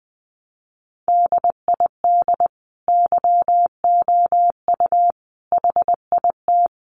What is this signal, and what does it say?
Morse code, heard as a keyed tone, about 20 wpm, 700 Hz. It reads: DID YOU HIT
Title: Sentences from Top 1200 Words 20wpm